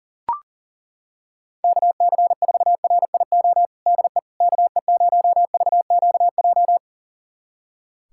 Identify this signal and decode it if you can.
Morse code, heard as a keyed tone, about 40 wpm, 700 Hz. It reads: KC4RIO DE KE0VQJ
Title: USA Callsign DE Callsign 40wpm